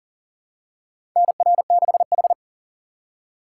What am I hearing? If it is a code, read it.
Morse code, heard as a keyed tone, about 38 wpm, 700 Hz. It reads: NR6H
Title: USA Call Signs - 2x Character Spacing - 40wpm